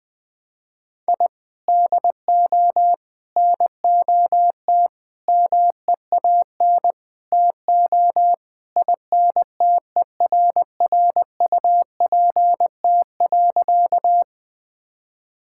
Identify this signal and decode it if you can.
Morse code, heard as a keyed tone, about 20 wpm, 700 Hz. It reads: I DO NOT MEAN TO INTERRUPT.
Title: Sentences with 5-Word Phrases - 20wpm